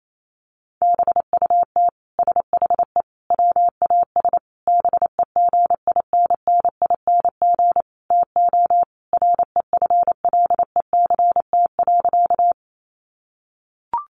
Morse code, heard as a keyed tone, about 28 wpm, 700 Hz. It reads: BUT SHE WAS BEGINNING TO REFLECT.
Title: Sentences from Top 1500 Words 28wpm